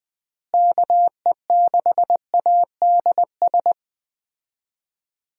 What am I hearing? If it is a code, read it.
Morse code, heard as a keyed tone, about 20 wpm, 700 Hz. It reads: KE6ADS
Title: USA Call Signs - 6 Characters - Repeated 2x - 20wpm